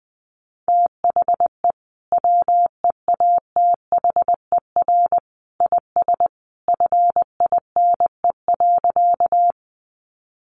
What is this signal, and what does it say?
Morse code, heard as a keyed tone, about 20 wpm, 700 Hz. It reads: THE WEATHER IS FINE.
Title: Sentences from Top 800 Words 20wpm